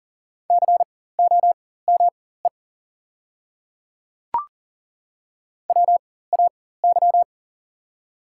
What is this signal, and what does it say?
Morse code, heard as a keyed tone, about 40 wpm, spaced as for 19 wpm, 700 Hz. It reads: COME WAY
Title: 4x Character Spacing - Top 100 Words - 40wpm